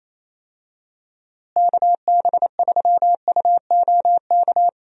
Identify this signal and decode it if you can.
Morse code, heard as a keyed tone, about 28 wpm, 700 Hz. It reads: KB3UOK
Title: USA Call Signs - 6 Characters - Repeated 3x - 28wpm